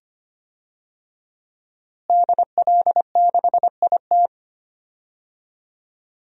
Morse code, heard as a keyed tone, about 25 wpm, 700 Hz. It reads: DL6IT
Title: DXCC-100 Call Signs - 5 Characters - Repeated 2x - 25wpm